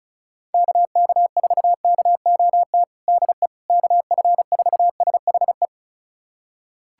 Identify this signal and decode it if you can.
Morse code, heard as a keyed tone, about 34 wpm, 700 Hz. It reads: KK4KOT DE KF4SHE
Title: USA Callsign DE Callsign 35wpm